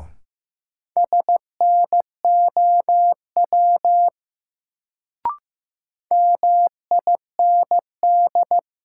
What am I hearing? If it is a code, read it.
Morse code, heard as a keyed tone, about 15 wpm, 700 Hz. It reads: SNOW MIND
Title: Sentences from Top 500 Words - New Word Drill - 15wpm